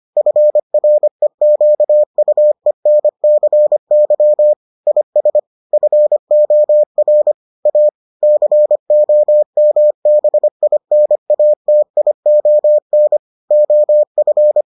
Morse code, heard as a keyed tone, about 25 wpm, 600 Hz. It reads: FREQUENCY IS FOR A COMBINATION OF